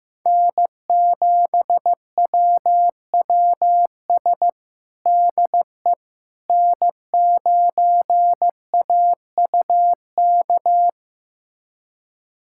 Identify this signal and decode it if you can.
Morse code, heard as a keyed tone, about 15 wpm, 700 Hz. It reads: N7WWS DE N9AUK
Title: USA Callsign DE Callsign - 5 Characters - 15wpm